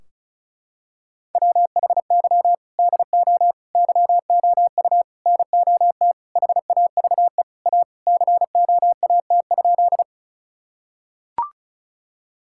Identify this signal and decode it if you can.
Morse code, heard as a keyed tone, about 34 wpm, 700 Hz. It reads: WHY DO YOU NOT HAVE A COAT?
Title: Sentences from Top 1600 Words 35wpm